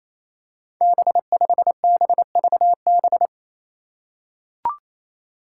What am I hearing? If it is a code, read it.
Morse code, heard as a keyed tone, about 28 wpm, 700 Hz. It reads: B5BVB
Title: Five Challenging Letter-Number Combinations 28wpm